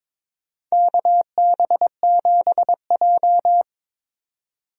Morse code, heard as a keyed tone, about 22 wpm, 700 Hz. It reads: KB7J
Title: USA Call Signs - 4 Characters - 22wpm